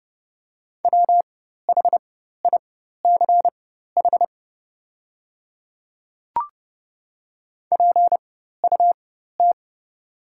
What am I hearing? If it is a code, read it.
Morse code, heard as a keyed tone, about 30 wpm, spaced as for 14 wpm, 690 Hz. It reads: WHICH PUT
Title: 4x Character Spacing - Top 300 Words - 30wpm